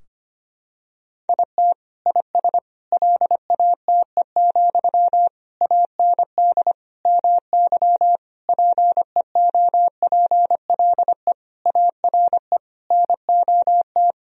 Morse code, heard as a keyed tone, about 25 wpm, 700 Hz. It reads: IT IS LATE, AND MY PEOPLE ARE NOT
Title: Sentences from Top 1000 Words - Review Part 3 - 25wpm